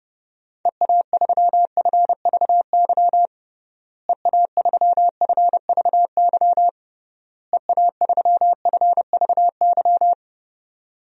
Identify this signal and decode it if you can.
Morse code, heard as a keyed tone, about 32 wpm, 700 Hz. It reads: EA3FVY EA3FVY EA3FVY
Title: DXCC-100 Call Signs - 6 Characters - Repeated 3x - 30wpm